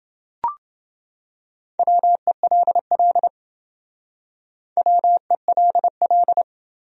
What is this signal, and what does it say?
Morse code, heard as a keyed tone, about 28 wpm, 700 Hz. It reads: WELL WELL